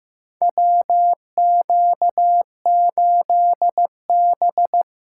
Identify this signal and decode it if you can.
Morse code, heard as a keyed tone, about 15 wpm, 700 Hz. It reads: WQ8B
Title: USA Call Signs - 2x Character Spacing - 15wpm